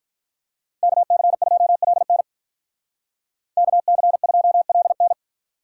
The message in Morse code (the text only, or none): KC2LN KC2LN